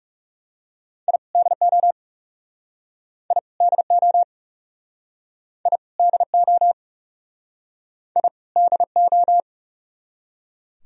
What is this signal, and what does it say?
Morse code, heard as a keyed tone, about 37 wpm, 700 Hz. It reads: I DO I DO I DO I DO
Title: Speed-Racing - Sets of 2 Words 30wpm